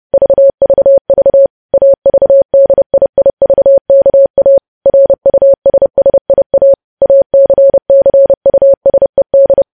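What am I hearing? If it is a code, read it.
Morse code, heard as a keyed tone, about 30 wpm, 550 Hz. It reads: VVV AVDIIVKA RUSSIA ACCUSED
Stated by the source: Morse code news headlines practice sent at 30 wpm.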